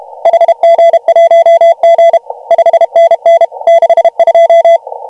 Morse code, heard as a keyed tone, about 30 wpm, 690 Hz. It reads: HG1G 5NN 62
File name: AUDIO2019 / HG1G CW